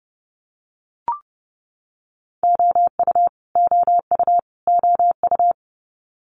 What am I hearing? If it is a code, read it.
Morse code, heard as a keyed tone, about 30 wpm, 700 Hz. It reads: OU OU OU
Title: Multi-letter Phonemes Repeated 3x 30wpm